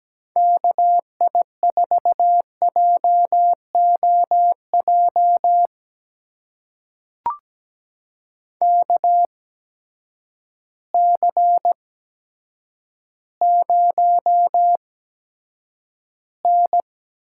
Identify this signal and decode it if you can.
Morse code, heard as a keyed tone, about 17 wpm, spaced as for 8 wpm, 700 Hz. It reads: KI4JOJ KC0N
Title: USA Call Signs - 6 Characters - 8x Character Spacing - 17wpm